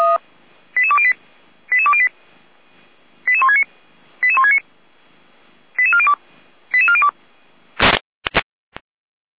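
Five-tone selective call (ZVEI), heard as quick runs of five tones, tones of about 80 ms. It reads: 88208 88208 88169 88169 88402 88402
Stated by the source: Five-tone ZVEI selective call received on an SDR.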